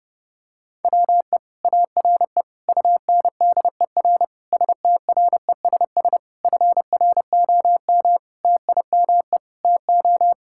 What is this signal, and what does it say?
Morse code, heard as a keyed tone, about 30 wpm, 700 Hz. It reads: WE ARE UNDER STRESS FROM TIME TO
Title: Sentences from Top 1000 Words - Review Part 6 - 30wpm